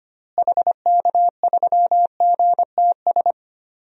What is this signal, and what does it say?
Morse code, heard as a keyed tone, about 25 wpm, 700 Hz. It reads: HK3GTS
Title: DXCC-100 Call Signs - 6 Characters - Repeated 3x - 25wpm